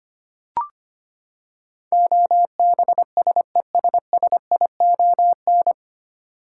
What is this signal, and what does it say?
Morse code, heard as a keyed tone, about 25 wpm, 700 Hz. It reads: OBSESSION
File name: Nine Letter Words - 25wpm